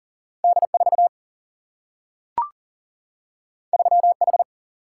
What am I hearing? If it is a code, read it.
Morse code, heard as a keyed tone, about 39 wpm, 700 Hz. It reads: D 4 3 H
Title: Single-Letter Single-Number 40wpm